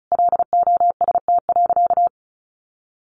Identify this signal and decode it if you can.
Morse code, heard as a keyed tone, about 34 wpm, 700 Hz. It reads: LOST.
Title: Sentences from Top 1400 Words 35wpm